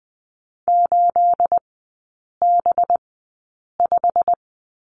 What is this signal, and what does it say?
Morse code, heard as a keyed tone, about 20 wpm, spaced as for 8 wpm, 690 Hz. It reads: 8B5
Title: 212 Lesson 34e - Number 8 - Three Characters 20wpm